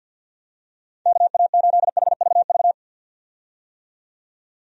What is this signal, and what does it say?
Morse code, heard as a keyed tone, about 52 wpm, 700 Hz. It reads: KA8HVV
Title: USA Call Signs - 6 Characters - 50wpm